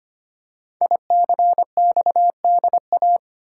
Morse code, heard as a keyed tone, about 25 wpm, 700 Hz. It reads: ICXDA
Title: Code Groups - 5 Letter:Numbers - 25wpm